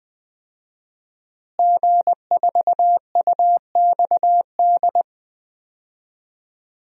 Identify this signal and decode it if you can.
Morse code, heard as a keyed tone, about 20 wpm, 700 Hz. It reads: G4UXD
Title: DXCC-100 Call Signs - 5 Characters - Repeated 2x - 20wpm